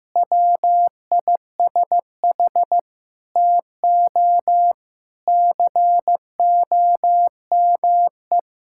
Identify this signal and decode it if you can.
Morse code, heard as a keyed tone, about 15 wpm, 700 Hz. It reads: WISH TO COME
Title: Sentences from Top 1600 Words 15wpm